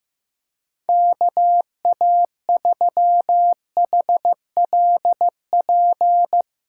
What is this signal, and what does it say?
Morse code, heard as a keyed tone, about 15 wpm, 700 Hz. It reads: KA3HLP
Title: USA Callsign DE Callsign 15wpm